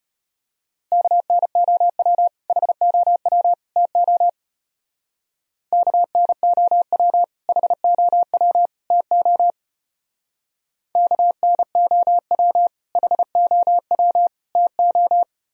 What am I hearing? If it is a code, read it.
Morse code, heard as a keyed tone, about 32 wpm, 700 Hz. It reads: KNOW HOW TO KNOW HOW TO KNOW HOW TO